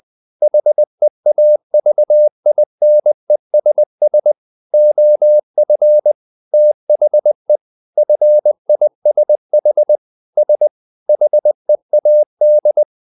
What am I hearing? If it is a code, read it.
Morse code, heard as a keyed tone, about 20 wpm, 600 Hz. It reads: HEAVINESS OF THE FISH S HEAD
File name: the old man and the sea part 2 20wpm